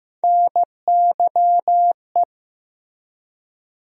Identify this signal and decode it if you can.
Morse code, heard as a keyed tone, about 15 wpm, 700 Hz. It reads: NYE
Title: USA Call Sign Postfixes - 3 Letters 15wpm